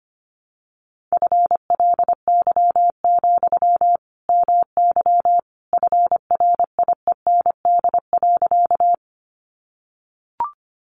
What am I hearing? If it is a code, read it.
Morse code, heard as a keyed tone, about 25 wpm, 700 Hz. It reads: FLY, MY FRIEND.
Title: Sentences from Top 500 Words 25wpm